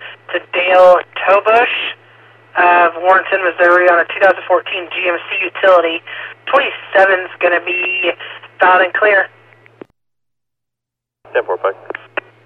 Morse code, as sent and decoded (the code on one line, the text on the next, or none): none